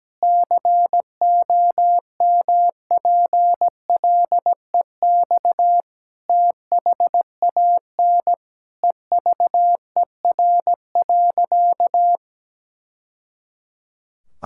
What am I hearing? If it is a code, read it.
Morse code, heard as a keyed tone, about 17 wpm, 700 Hz. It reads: COMPLEX THAN EVER.
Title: Sentences from Top 1000 Words - Review Part 6 - 17wpm